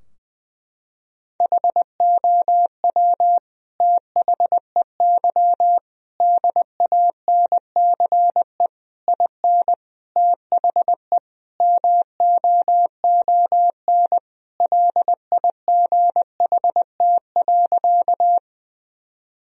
Morse code, heard as a keyed tone, about 20 wpm, 700 Hz. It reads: HOW THEY DANCE IN THE MOON LIGHT.
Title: Sentences from Top 1000 Words - Review Part 4 - 20wpm